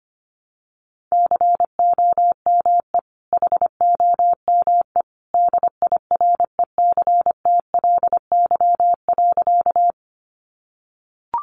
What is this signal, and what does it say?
Morse code, heard as a keyed tone, about 25 wpm, 700 Hz. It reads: COME HOME DIRECTLY.
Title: Sentences from Top 800 Words 25wpm